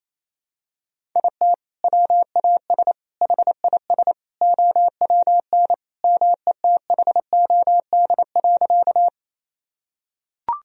Morse code, heard as a keyed tone, about 28 wpm, 700 Hz. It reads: IT WAS HIS OWN METHOD.
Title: Sentences from Top 1000 Words - Review Part 3 - 28wpm